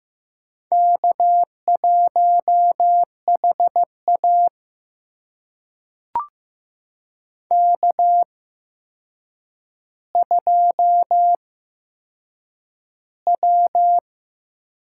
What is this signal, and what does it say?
Morse code, heard as a keyed tone, about 15 wpm, spaced as for 7 wpm, 700 Hz. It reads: K1HA K2W
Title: USA Call Signs - 4 Characters - 8x Character Spacing - 15wpm